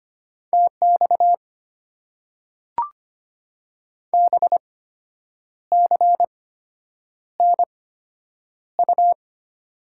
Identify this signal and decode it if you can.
Morse code, heard as a keyed tone, about 25 wpm, spaced as for 8 wpm, 700 Hz. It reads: TX BCNU
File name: Top 90 QSO Elements - 8x Character Spacing - 25wpm